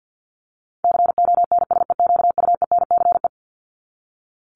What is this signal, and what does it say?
Morse code, heard as a keyed tone, about 50 wpm, 700 Hz. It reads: CONSEQUENCE
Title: LY Suffix - Top 100 Words with Base Words 50wpm